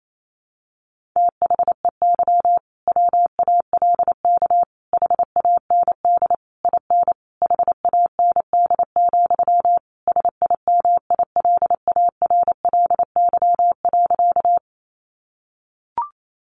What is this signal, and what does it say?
Morse code, heard as a keyed tone, about 28 wpm, 700 Hz. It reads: THEY WALK HAND IN HAND, SIMILARLY.
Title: Sentences from Top 1400 Words 28wpm